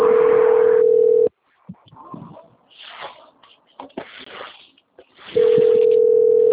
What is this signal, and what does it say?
Ringback tone (North America), 440 + 480 Hz.
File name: out- - - - - .35827